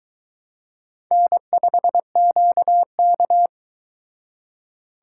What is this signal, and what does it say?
Morse code, heard as a keyed tone, about 22 wpm, 700 Hz. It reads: N5QK